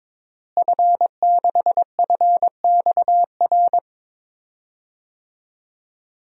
Morse code, heard as a keyed tone, about 22 wpm, 700 Hz. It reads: F6FXR
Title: DXCC-100 Call Signs - 5 Characters - Repeated 2x - 22wpm